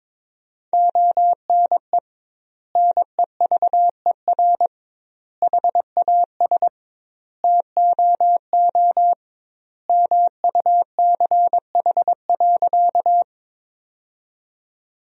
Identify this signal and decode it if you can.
Morse code, heard as a keyed tone, about 22 wpm, 700 Hz. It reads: ONE NEVER HAS TOO MUCH.
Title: 2x Word-Spacing - Sentences from Top 300 Words - 22wpm